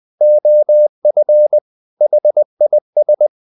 Morse code, heard as a keyed tone, about 20 wpm, 600 Hz. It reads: OF HIS
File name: the old man and the sea part 2 20wpm